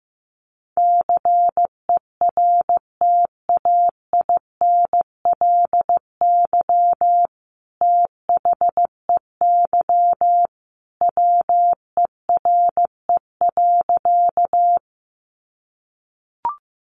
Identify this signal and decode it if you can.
Morse code, heard as a keyed tone, about 15 wpm, 700 Hz. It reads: CERTAINLY THEY WERE.
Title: Sentences from Top 900 Words 15wpm